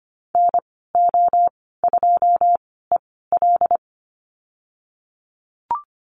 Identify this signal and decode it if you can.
Morse code, heard as a keyed tone, about 25 wpm, spaced as for 16 wpm, 700 Hz. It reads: NO2EL
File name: Lesson 15g - Number 5 - Call Signs 25wpm